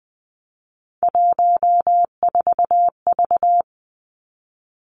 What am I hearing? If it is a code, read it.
Morse code, heard as a keyed tone, about 20 wpm, 700 Hz. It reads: 14V
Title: Three Challenging Letter-Number Combinations 20wpm